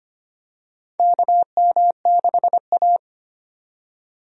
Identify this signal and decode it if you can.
Morse code, heard as a keyed tone, about 25 wpm, 700 Hz. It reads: KM6A
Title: USA Call Signs - 4 Characters - 25wpm